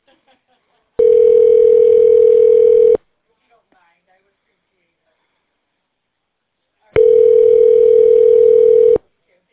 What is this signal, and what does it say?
Ringback tone (North America), 440 + 480 Hz.